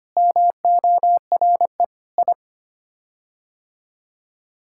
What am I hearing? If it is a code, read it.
Morse code, heard as a keyed tone, about 25 wpm, 700 Hz. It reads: MORE I
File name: Sets of 9 Words - 25wpm